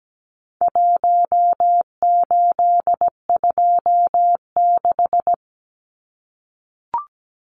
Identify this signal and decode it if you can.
Morse code, heard as a keyed tone, about 17 wpm, 700 Hz. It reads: 1826
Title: Numbers - Contest Style - 4-Digit 17wpm